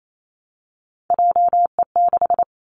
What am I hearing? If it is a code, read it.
Morse code, heard as a keyed tone, about 28 wpm, 700 Hz. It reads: JE6
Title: Three Challenging Letter-Number Combinations 28wpm